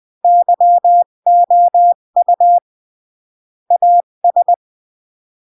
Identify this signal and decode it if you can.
Morse code, heard as a keyed tone, about 20 wpm, 700 Hz. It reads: YOU AS
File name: CW / jodorowsky / 82 commandments 20-15